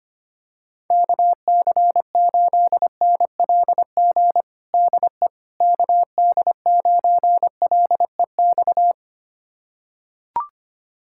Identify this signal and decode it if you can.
Morse code, heard as a keyed tone, about 25 wpm, 700 Hz. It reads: KC8NLG DE KD9LEX
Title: USA Callsign DE Callsign - 6 Characters - 25wpm